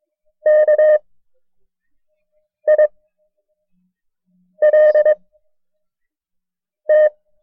Morse code, heard as a keyed tone, about 22 wpm, 610 Hz. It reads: K I L T
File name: FreeFiles / CW-practice / KB6MT-tape1-side2